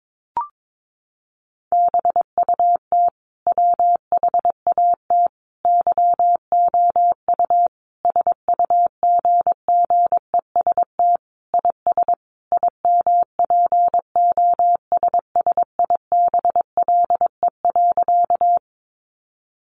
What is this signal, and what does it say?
Morse code, heard as a keyed tone, about 22 wpm, 700 Hz. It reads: BUT WHAT YOU SUGGEST IS IMPOSSIBLE.
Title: Sentences from Top 1300 Words 22wpm